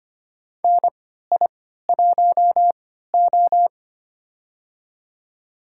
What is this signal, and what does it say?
Morse code, heard as a keyed tone, about 25 wpm, spaced as for 14 wpm, 700 Hz. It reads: NI1O
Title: USA Call Signs - 4 Characters - 3x Character Spacing - 25wpm